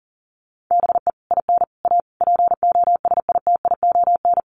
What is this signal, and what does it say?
Morse code, heard as a keyed tone, about 40 wpm, 700 Hz. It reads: BE IN A POSITION
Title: Sets of 5 Words Encore2 40wpm